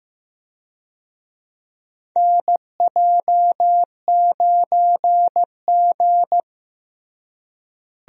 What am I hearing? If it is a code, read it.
Morse code, heard as a keyed tone, about 15 wpm, 700 Hz. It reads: NJ9G
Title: USA Call Signs - 4 Characters - Repeated 2x - 15wpm